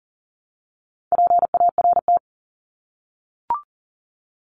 Morse code, heard as a keyed tone, about 40 wpm, 700 Hz. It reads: PART
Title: ES:S Suffix - Top 100 Words with Base Words 40wpm